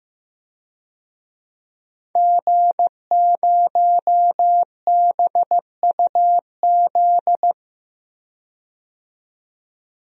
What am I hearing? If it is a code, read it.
Morse code, heard as a keyed tone, about 15 wpm, 700 Hz. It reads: G0BUZ
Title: DXCC-100 Call Signs - 5 Characters - 15wpm